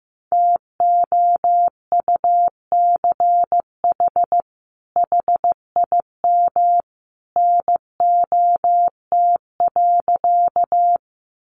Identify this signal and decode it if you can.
Morse code, heard as a keyed tone, about 15 wpm, 700 Hz. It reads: TOUCH HIM NOT.
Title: Sentences from Top 1200 Words 15wpm